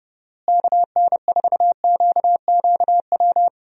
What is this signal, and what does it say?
Morse code, heard as a keyed tone, about 29 wpm, 700 Hz. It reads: KN4QQW
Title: USA Call Signs - 6 Characters - Repeated 2x - 30wpm